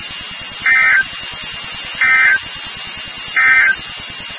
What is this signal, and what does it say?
Emergency Alert System (SAME) tones, heard as harsh buzzing bursts. It is heard from 0.5 to 3.8 s.
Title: NOAA Weather Radio WXK36